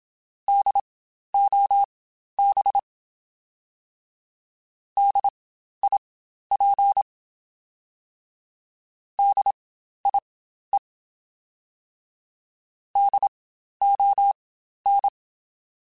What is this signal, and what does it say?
Morse code, heard as a keyed tone, about 27 wpm, spaced as for 12 wpm, 800 Hz. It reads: DOB DIP DIE DON
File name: Spaced Alphas / Spaced Alphas mp3s / Ds sp @25